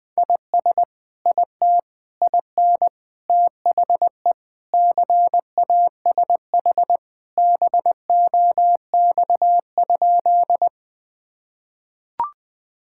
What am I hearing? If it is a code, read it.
Morse code, heard as a keyed tone, about 20 wpm, 700 Hz. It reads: IS IT IN THE CASH BOX?
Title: Sentences from Top 1000 Words - Review Part 5 - 20wpm